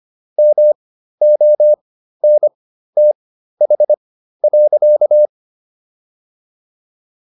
Morse code, heard as a keyed tone, about 25 wpm, spaced as for 13 wpm, 600 Hz. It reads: MONTH.